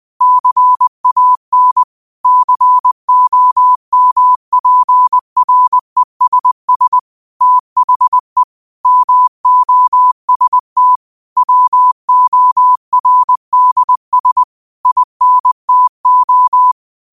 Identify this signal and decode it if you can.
Morse code, heard as a keyed tone, about 20 wpm, 1000 Hz. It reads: CAN COMPRESS THE MOST WORDS INTO